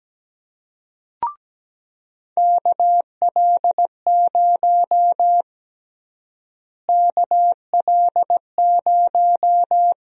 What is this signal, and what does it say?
Morse code, heard as a keyed tone, about 17 wpm, 700 Hz. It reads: KL0 KL0
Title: USA Call Sign Prefixes - Repeated 2x - 17wpm